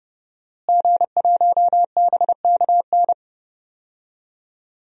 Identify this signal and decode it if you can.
Morse code, heard as a keyed tone, about 30 wpm, 700 Hz. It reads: G1BKN
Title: DXCC-100 Call Signs - 5 Characters - Repeated 3x - 30wpm